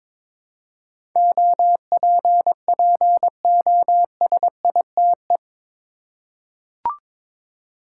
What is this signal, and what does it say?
Morse code, heard as a keyed tone, about 21 wpm, 700 Hz. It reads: OPPOSITE
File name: practice / 22wpm / Eight Letter Words - 22wpm